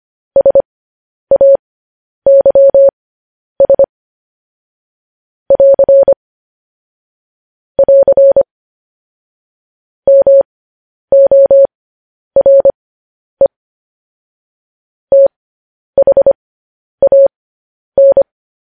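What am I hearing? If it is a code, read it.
Morse code, heard as a keyed tone, about 25 wpm, spaced as for 10 wpm, 550 Hz. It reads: SAYS <AR> <AR> MORE THAN
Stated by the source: Morse code news headlines at 10 wpm.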